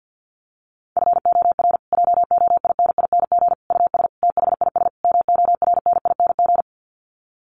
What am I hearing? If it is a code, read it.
Morse code, heard as a keyed tone, about 50 wpm, 700 Hz. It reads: FOR JOINING US THIS MORNING
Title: Sets of 5 Words Encore 50wpm